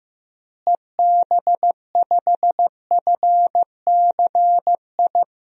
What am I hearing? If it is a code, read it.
Morse code, heard as a keyed tone, about 15 wpm, 700 Hz. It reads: EB5FCI
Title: DXCC-100 Call Signs - 6 Characters - Repeated 3x - 15wpm